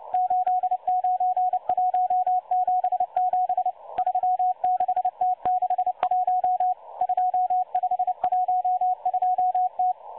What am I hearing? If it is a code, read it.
Morse code, heard as a keyed tone, about 29 wpm, 730 Hz. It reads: 89177 36T61 2512T
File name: M12-Morse-Numbers-Station-8116-kHz-CW